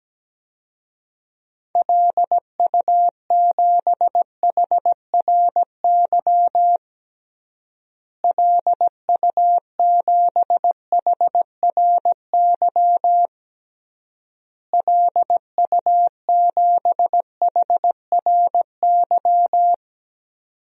Morse code, heard as a keyed tone, about 17 wpm, 700 Hz. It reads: LU7HRY LU7HRY LU7HRY
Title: DXCC-100 Call Signs - 6 Characters - Repeated 3x - 17wpm